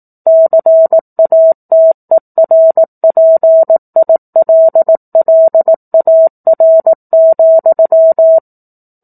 Morse code, heard as a keyed tone, about 18 wpm, 650 Hz. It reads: CATERPILLAR,
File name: 04-Chapter-04 18 WPM